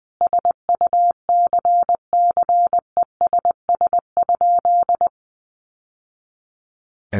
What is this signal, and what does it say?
Morse code, heard as a keyed tone, about 20 wpm, 700 Hz. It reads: SUCCESS?
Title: Sentences from Top 900 Words 20wpm